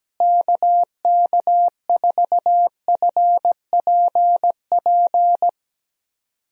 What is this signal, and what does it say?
Morse code, heard as a keyed tone, about 17 wpm, 700 Hz. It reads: KK4FPP
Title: USA Call Signs - 6 Characters - Repeated 3x - 17wpm